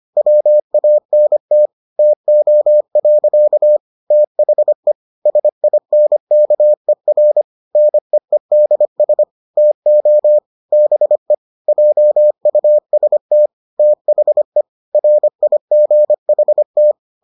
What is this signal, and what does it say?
Morse code, heard as a keyed tone, about 25 wpm, 600 Hz. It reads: WANT TO. THE SINKER NEEDS TO BE JUST THE RIGHT